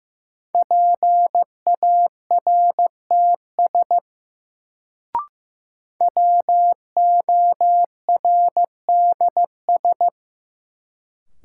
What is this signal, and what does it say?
Morse code, heard as a keyed tone, about 16 wpm, 700 Hz. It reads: PARTS WORDS
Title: Sets of 3 Words - New Word Drill - 15wpm